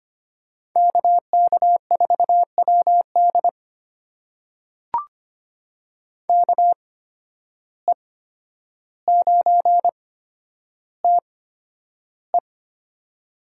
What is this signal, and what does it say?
Morse code, heard as a keyed tone, about 25 wpm, spaced as for 11 wpm, 700 Hz. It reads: KK4WD KE9TE
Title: USA Call Signs - 5 Characters - 8x Character Spacing - 25wpm